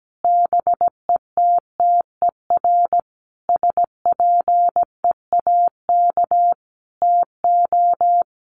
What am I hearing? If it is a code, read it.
Morse code, heard as a keyed tone, about 17 wpm, 700 Hz. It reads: BETTER SPEAK TO
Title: Sentences from Top 1100 Words 17wpm